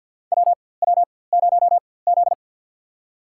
Morse code, heard as a keyed tone, about 53 wpm, spaced as for 24 wpm, 700 Hz. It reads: WW0Z
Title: USA Call Signs - 4x Character Spacing - 50wpm